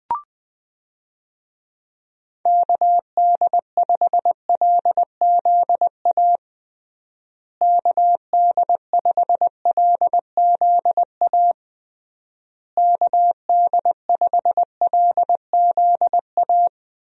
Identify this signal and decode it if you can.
Morse code, heard as a keyed tone, about 20 wpm, 700 Hz. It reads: KD5LZA KD5LZA KD5LZA
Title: USA Call Signs - 6 Characters - Repeated 3x - 20wpm